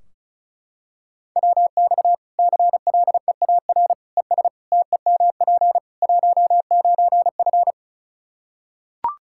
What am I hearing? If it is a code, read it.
Morse code, heard as a keyed tone, about 34 wpm, 700 Hz. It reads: WX CLEAR ES TEMP 19F
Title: Practice QSO - Exchange 2 - Info - 35wpm